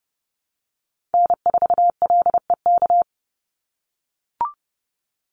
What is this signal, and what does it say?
Morse code, heard as a keyed tone, about 30 wpm, 700 Hz. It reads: N4LEK
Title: USA Call Signs 30wpm